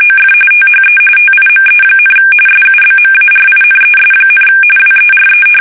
RTTY, about 45.45 baud. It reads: OF THE TIME, SUBMARINES, WERE
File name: fsk850 / usnavy3